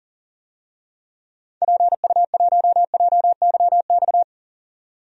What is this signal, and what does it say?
Morse code, heard as a keyed tone, about 40 wpm, 700 Hz. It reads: PU1JYX
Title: DXCC-100 Call Signs - 6 Characters - 40wpm